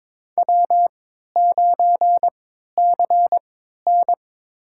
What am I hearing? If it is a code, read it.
Morse code, heard as a keyed tone, about 22 wpm, spaced as for 12 wpm, 690 Hz. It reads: W9CN
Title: USA Call Signs - 4 Characters - 3x Character Spacing - 22wpm